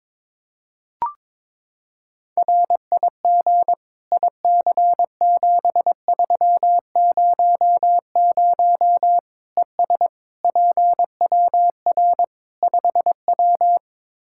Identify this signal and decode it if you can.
Morse code, heard as a keyed tone, about 22 wpm, 700 Hz. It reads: RIG IC7300 ES PWR 5W
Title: Practice QSO - Exchange 2 - Info - 22wpm